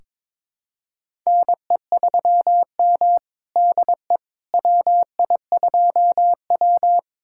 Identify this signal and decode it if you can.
Morse code, heard as a keyed tone, about 21 wpm, 700 Hz. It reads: NE3M DE WI2W
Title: USA Callsign DE Callsign - 4 Characters - 22wpm